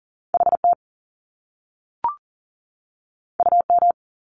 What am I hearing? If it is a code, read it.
Morse code, heard as a keyed tone, about 41 wpm, 700 Hz. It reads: HT UM
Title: Top Two Letter Combinations in Words 40wpm